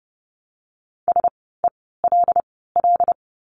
Morse code, heard as a keyed tone, about 30 wpm, spaced as for 17 wpm, 700 Hz. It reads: SELL
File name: Lesson 16f - Letter c - Words 30wpm